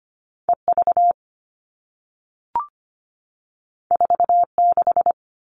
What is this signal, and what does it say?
Morse code, heard as a keyed tone, about 25 wpm, 700 Hz. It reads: E V 4 6
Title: All Cut Numbers - 2-Digit 25wpm